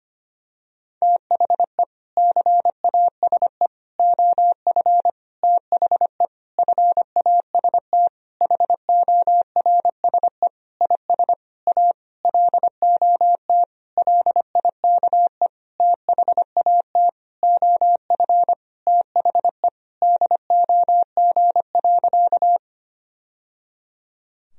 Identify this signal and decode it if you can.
Morse code, heard as a keyed tone, about 25 wpm, 700 Hz. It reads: THE CASE OF THE FAST HORSE IS A LOT LIKE THAT OF THE DOG.
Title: Sentences from Top 1000 Words - Review Part 2 - 25wpm